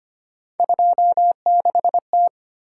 Morse code, heard as a keyed tone, about 25 wpm, 700 Hz. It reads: 26T
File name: Three Letter-Number Combinations - 25wpm